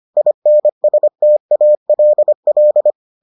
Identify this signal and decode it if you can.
Morse code, heard as a keyed tone, about 25 wpm, 600 Hz. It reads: INSTALL